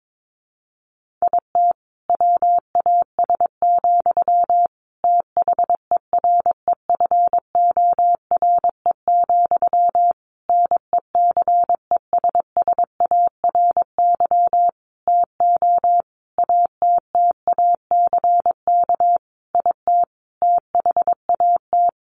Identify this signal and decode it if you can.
Morse code, heard as a keyed tone, about 22 wpm, 700 Hz. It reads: IT WAS, THEREFORE, NECESSARY TO ATTACK IT THAT
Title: Sentences from Top 1500 Words 22wpm